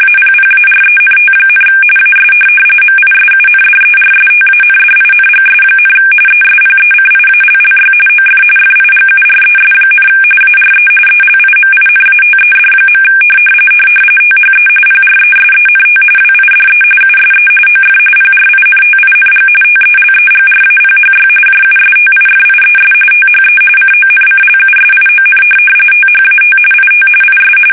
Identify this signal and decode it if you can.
RTTY, about 45.45 baud. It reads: APPARENT, AS MISUNDERSTANDINGS, ESPECIALLY OF FIGURES, ARE ELIMINATED. THE PRESENT WIDE USE OF PRINTING TELEGRAPHY JUSTIFIES THE CONCLUSION THAT IT IS
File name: fsk850